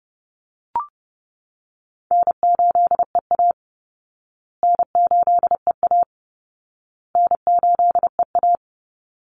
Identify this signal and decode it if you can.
Morse code, heard as a keyed tone, about 30 wpm, 700 Hz. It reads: N8EA N8EA N8EA
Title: Call Signs Repeated 2x - 30wpm